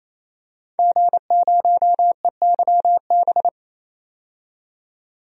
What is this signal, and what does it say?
Morse code, heard as a keyed tone, about 28 wpm, 700 Hz. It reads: G0EYB
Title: DXCC-100 Call Signs - 5 Characters - 28wpm